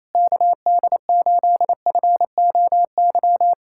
Morse code, heard as a keyed tone, about 28 wpm, 700 Hz. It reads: KD8FOY
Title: USA Callsign DE Callsign 28wpm